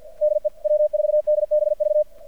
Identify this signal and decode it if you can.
Morse code, heard as a keyed tone, about 50 wpm, 600 Hz. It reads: DE W4DBV
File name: archived / CRSnet